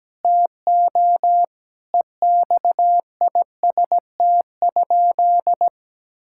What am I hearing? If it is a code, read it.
Morse code, heard as a keyed tone, about 17 wpm, 700 Hz. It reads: TO EXIST?
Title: Sentences from Top 1000 Words - Review Part 5 - 17wpm